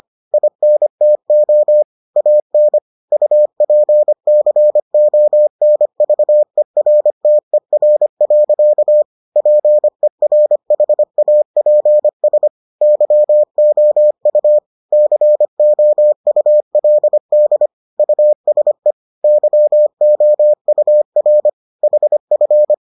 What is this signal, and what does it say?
Morse code, heard as a keyed tone, about 25 wpm, 600 Hz. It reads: INTO AN UPCONVERTER. PERHAPS YOU COULD USE YOUR HF